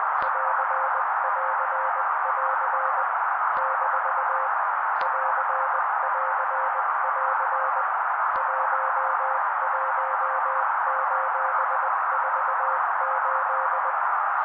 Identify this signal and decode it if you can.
Morse code, heard as a keyed tone, about 20 wpm, 530 Hz. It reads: <AR><AR><AR> - <AR><AR><AR> 11:48
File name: daily / 18wpm noise